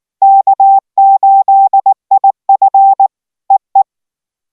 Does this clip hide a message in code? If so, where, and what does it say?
Yes: Morse code from 0.2 to 3.8 s, heard as a keyed tone, about 19 wpm, 780 Hz. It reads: K8IF EE